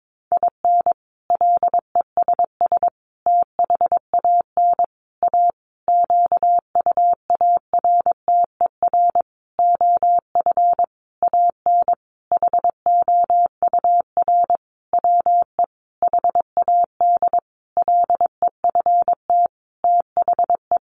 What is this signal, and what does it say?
Morse code, heard as a keyed tone, about 22 wpm, 700 Hz. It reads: IN LESS THAN A QUARTER OF AN HOUR WE HAD LEFT THE
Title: Sentences from Top 1300 Words 22wpm